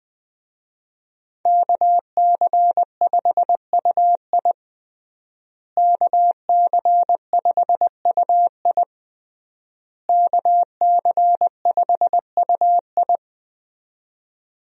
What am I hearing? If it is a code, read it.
Morse code, heard as a keyed tone, about 21 wpm, 700 Hz. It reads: KC5UI KC5UI KC5UI
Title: USA Call Signs - 5 Characters - Repeated 3x - 20wpm